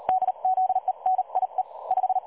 Morse code, heard as a keyed tone, about 38 wpm, spaced as for 33 wpm, 750 Hz. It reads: DZENIE 5
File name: dlazaawansowanych / nagraniazpasma / websdr recording 3565.0kHz